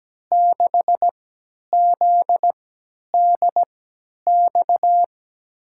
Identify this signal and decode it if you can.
Morse code, heard as a keyed tone, about 17 wpm, spaced as for 10 wpm, 700 Hz. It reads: 6ZDX
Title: USA Call Signs - 6 Characters - 3x Character Spacing - 17wpm